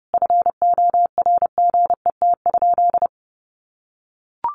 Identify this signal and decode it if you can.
Morse code, heard as a keyed tone, about 30 wpm, 700 Hz. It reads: FORGET?
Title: Sentences from Top 1200 Words 30wpm